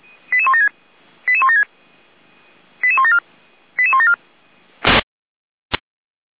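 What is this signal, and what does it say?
Five-tone selective call (ZVEI), heard as quick runs of five tones, tones of about 70 ms. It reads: 88176 88176 88174 88174
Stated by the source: ZVEI five-tone selective call received on an SDR.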